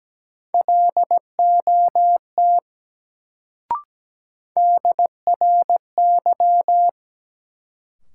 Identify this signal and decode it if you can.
Morse code, heard as a keyed tone, about 17 wpm, 700 Hz. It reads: LOT DRY
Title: Sentences from Top 500 Words - New Word Drill - 17wpm